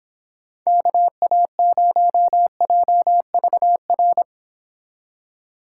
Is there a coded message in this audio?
Yes: Morse code from 0.7 to 4.2 s, heard as a keyed tone, about 29 wpm, 700 Hz. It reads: KA0JVR